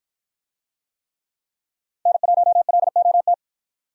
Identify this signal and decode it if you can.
Morse code, heard as a keyed tone, about 53 wpm, 700 Hz. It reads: N1LOT